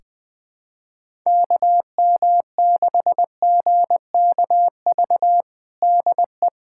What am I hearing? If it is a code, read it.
Morse code, heard as a keyed tone, about 20 wpm, 700 Hz. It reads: KM6GKV DE
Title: USA Callsign DE Callsign 20wpm